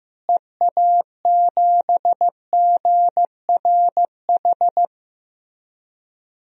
Morse code, heard as a keyed tone, about 15 wpm, 700 Hz. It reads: EA7GRH
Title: DXCC-100 Call Signs - 6 Characters - Repeated 2x - 15wpm